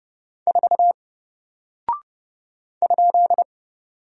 Morse code, heard as a keyed tone, about 30 wpm, 700 Hz. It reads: 4 ?
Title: Single Character (Letter:Number:Punctuation:Prosign) 30wpm